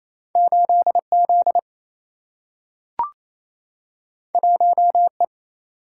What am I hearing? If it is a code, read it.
Morse code, heard as a keyed tone, about 28 wpm, 700 Hz. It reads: 8 Z 1 E
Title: Single-Letter Single-Number 28wpm